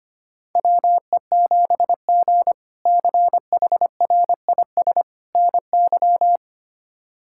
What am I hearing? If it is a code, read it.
Morse code, heard as a keyed tone, about 25 wpm, 700 Hz. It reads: WE7G CHRIS NY
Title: K1USN SST QSO Practice - 25wpm